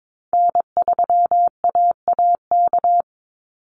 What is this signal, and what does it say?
Morse code, heard as a keyed tone, about 22 wpm, 700 Hz. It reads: N3AAK
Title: Call Signs Repeated 3x - 22wpm